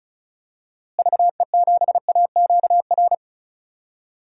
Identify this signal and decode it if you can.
Morse code, heard as a keyed tone, about 34 wpm, 700 Hz. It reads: VE7AQR
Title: DXCC-100 Call Signs - 6 Characters - Repeated 2x - 35wpm